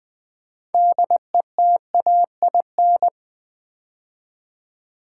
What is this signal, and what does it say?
Morse code, heard as a keyed tone, about 20 wpm, 700 Hz. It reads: DETAIN
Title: 6 Letter Words - First 3 Letters Given - 20wpm